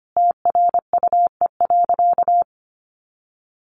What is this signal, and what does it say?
Morse code, heard as a keyed tone, about 25 wpm, 700 Hz. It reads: TRUE.
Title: Sentences from Top 500 Words 25wpm